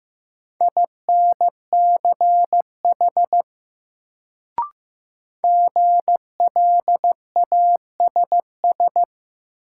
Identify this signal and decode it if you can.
Morse code, heard as a keyed tone, about 16 wpm, 700 Hz. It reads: INCH GLASS
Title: Sentences from Top 1000 Words - Review Part 3 - New Word Drill - 15wpm